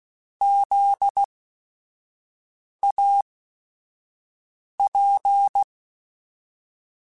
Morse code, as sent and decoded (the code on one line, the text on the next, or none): --.. .- .--.
ZAP